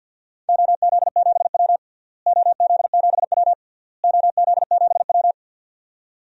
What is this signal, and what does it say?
Morse code, heard as a keyed tone, about 52 wpm, 700 Hz. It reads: OZ7W OZ7W OZ7W
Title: DXCC-100 Call Signs - 4 Characters - Repeated 3x - 50wpm